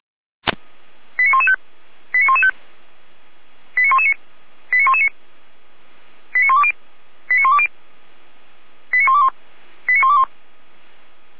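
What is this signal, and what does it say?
Five-tone selective call (ZVEI), heard as quick runs of five tones, tones of about 70 ms. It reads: 89115 89115 89119 89119 89120 89120 89121 89121
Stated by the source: ZVEI five-tone selective call received on an SDR by Feuerwehr Orlamünde.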